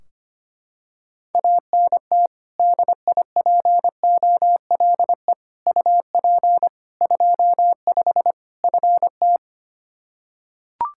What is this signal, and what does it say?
Morse code, heard as a keyed tone, about 25 wpm, 700 Hz. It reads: ANT DIPOLE UP 25 FT
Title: Practice QSO - Exchange 2 - Info - 25wpm